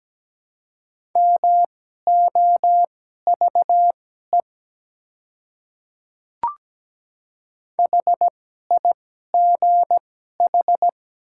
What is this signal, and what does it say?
Morse code, heard as a keyed tone, about 17 wpm, spaced as for 12 wpm, 700 Hz. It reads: MOVE HIGH
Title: 2x Character Spacing - Top 200 Words - 17wpm